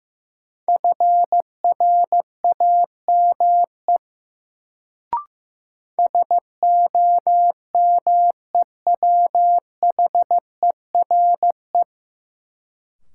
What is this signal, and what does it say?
Morse code, heard as a keyed tone, about 15 wpm, 700 Hz. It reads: FRAME SOMEWHERE
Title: Sentences from Top 1400 Words - New Word Drill Alt - 15wpm